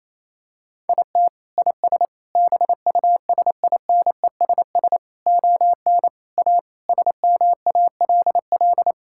Morse code, heard as a keyed tone, about 28 wpm, 700 Hz. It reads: IT IS BUSINESS ON A SMALL
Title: Sentences from Top 1000 Words - Review Part 5 - 28wpm